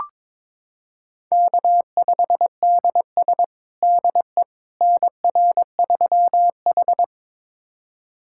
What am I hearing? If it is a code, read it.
Morse code, heard as a keyed tone, about 21 wpm, 700 Hz. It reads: K5DS DE NR3H
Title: USA Callsign DE Callsign - 4 Characters - 22wpm